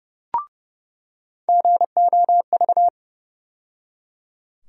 Morse code, heard as a keyed tone, about 30 wpm, 700 Hz. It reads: GOV
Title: USA Call Sign Postfixes - 3 Letters 30wpm